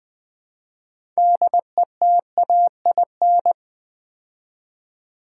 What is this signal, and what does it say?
Morse code, heard as a keyed tone, about 20 wpm, 700 Hz. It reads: DETAIN
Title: Build A Word - 2 Letters At A Time - 6-Letter Words - 20wpm